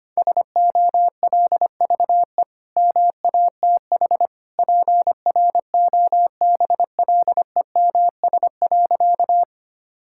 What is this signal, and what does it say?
Morse code, heard as a keyed tone, about 25 wpm, 700 Hz. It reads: SOLVE MATH PROBLEMS.
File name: Easy Sentences with Hints - 25wpm